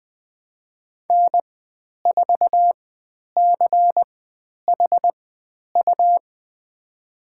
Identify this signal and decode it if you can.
Morse code, heard as a keyed tone, about 20 wpm, spaced as for 10 wpm, 700 Hz. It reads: N4CHU
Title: 102 Lesson 17g - Letter u - Call Signs 20wpm